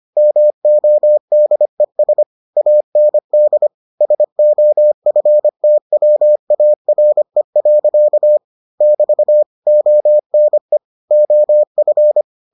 Morse code, heard as a keyed tone, about 25 wpm, 600 Hz. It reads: MODES AND SOFTWARE. <BT> ONE OF